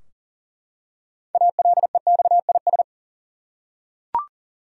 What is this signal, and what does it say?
Morse code, heard as a keyed tone, about 40 wpm, 700 Hz. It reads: ALEXIS
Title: USA Common Names - 40wpm